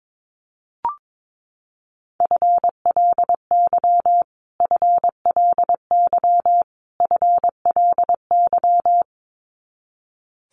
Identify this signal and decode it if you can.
Morse code, heard as a keyed tone, about 22 wpm, 700 Hz. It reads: FLY FLY FLY
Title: Top 500 Words Repeated 22wpm